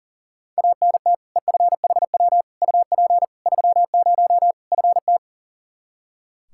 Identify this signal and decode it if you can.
Morse code, heard as a keyed tone, about 40 wpm, 700 Hz. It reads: ANT EFHW UP 30 FT
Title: Practice QSO - Exchange 2 - Info - 40wpm